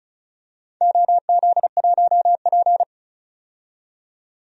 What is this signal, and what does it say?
Morse code, heard as a keyed tone, about 36 wpm, 700 Hz. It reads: OZ1P
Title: DXCC-100 Call Signs - 4 Characters - 35wpm